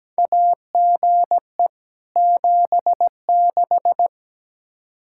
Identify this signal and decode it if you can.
Morse code, heard as a keyed tone, about 17 wpm, 700 Hz. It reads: AGE 76
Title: K1USN SST QSO Practice 17wpm